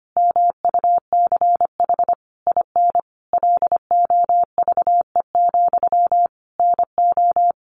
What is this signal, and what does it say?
Morse code, heard as a keyed tone, about 25 wpm, 700 Hz. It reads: MUCH IN LOVE, NO
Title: Sentences from Top 1500 Words 25wpm